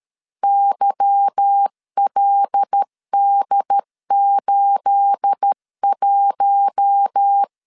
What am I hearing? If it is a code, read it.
Morse code, heard as a keyed tone, about 13 wpm, 800 Hz. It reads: YLD81
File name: LEZIONI / 9 28 12wpm 2351